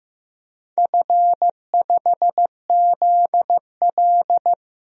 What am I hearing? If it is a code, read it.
Morse code, heard as a keyed tone, about 15 wpm, 700 Hz. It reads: F5ZL
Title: DXCC-100 Call Signs - 4 Characters - 15wpm